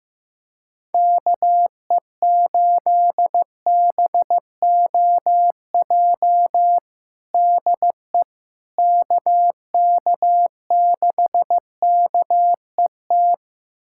Morse code, heard as a keyed tone, about 15 wpm, 700 Hz. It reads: KE8BOJ DE KK6KET
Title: USA Callsign DE Callsign - 6 Characters - 15wpm